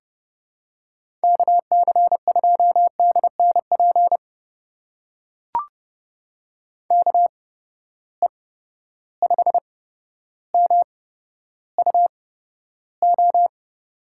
Morse code, heard as a keyed tone, about 30 wpm, spaced as for 13 wpm, 700 Hz. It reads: KC2DNP KE5MUO
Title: USA Call Signs - 6 Characters - 8x Character Spacing - 30wpm